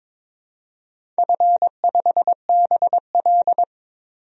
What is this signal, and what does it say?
Morse code, heard as a keyed tone, about 22 wpm, 700 Hz. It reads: F5BL
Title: DXCC-100 Call Signs - 4 Characters - 22wpm